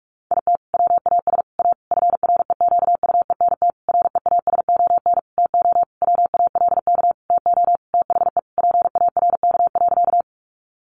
Morse code, heard as a keyed tone, about 44 wpm, 700 Hz. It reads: IT WAS A FREQUENT REASON TO WALK TO THE PARK.
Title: Sentences from Top 1400 Words 45wpm